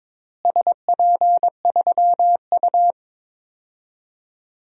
Morse code, heard as a keyed tone, about 21 wpm, 700 Hz. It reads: SP3U
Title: DXCC-100 Call Signs - 4 Characters - Repeated 3x - 22wpm